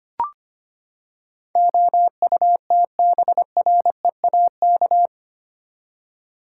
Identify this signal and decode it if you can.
Morse code, heard as a keyed tone, about 25 wpm, 700 Hz. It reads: OUTBREAK
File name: Eight Letter Words - 25wpm